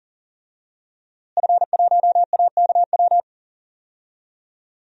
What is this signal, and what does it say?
Morse code, heard as a keyed tone, about 37 wpm, 700 Hz. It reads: F1AKW
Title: DXCC-100 Call Signs - 5 Characters - Repeated 3x - 40wpm